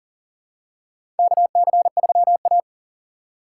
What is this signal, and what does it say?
Morse code, heard as a keyed tone, about 40 wpm, 700 Hz. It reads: KC3A
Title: USA Call Signs - 4 Characters - 4x Character Spacing - 40wpm